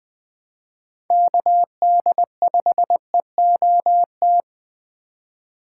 Morse code, heard as a keyed tone, about 20 wpm, 700 Hz. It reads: KD5EOT
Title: Speed Racing - USA 2x3 Call Signs 17wpm